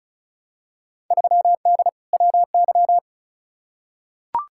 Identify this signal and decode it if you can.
Morse code, heard as a keyed tone, about 36 wpm, 700 Hz. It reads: 3D WY
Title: Field Day Exchanges Encore - 35wpm